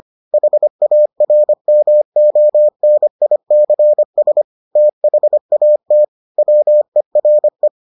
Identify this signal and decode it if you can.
Morse code, heard as a keyed tone, about 25 wpm, 600 Hz. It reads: HARMONICS THAT WERE